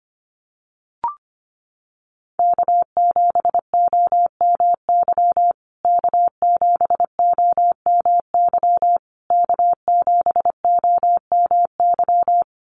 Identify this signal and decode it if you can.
Morse code, heard as a keyed tone, about 25 wpm, 700 Hz. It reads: K7OMY K7OMY K7OMY
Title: USA Call Signs Repeated 3x 25wpm